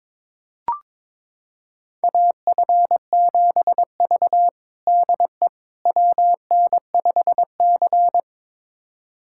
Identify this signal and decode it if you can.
Morse code, heard as a keyed tone, about 22 wpm, 700 Hz. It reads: AF7V DE WN5C
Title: USA Callsign DE Callsign - 4 Characters - 22wpm